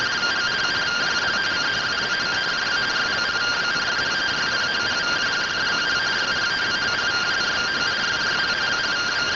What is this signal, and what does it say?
Two-tone teleprinter signal, about 170 Hz shift.
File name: findings / CT AXIAL -485mm 2.gen-20 rtty rx